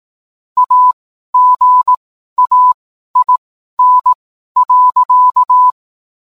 Morse code, heard as a keyed tone, about 18 wpm, spaced as for 12 wpm, 1000 Hz. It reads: AGAIN.